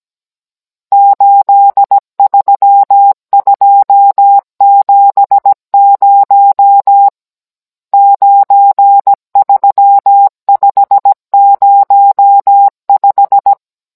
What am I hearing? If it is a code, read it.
Morse code, heard as a keyed tone, about 17 wpm, 800 Hz. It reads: 83270 93505
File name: Cours-NR19-16mm